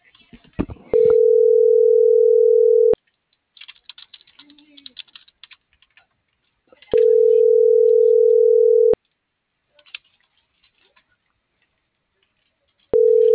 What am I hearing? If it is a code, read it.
Ringback tone (North America), 440 + 480 Hz.